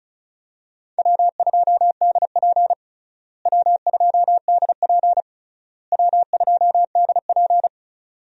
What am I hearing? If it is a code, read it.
Morse code, heard as a keyed tone, about 36 wpm, 700 Hz. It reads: W2DP W2DP W2DP
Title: USA Call Signs - 4 Characters - Repeated 3x - 35wpm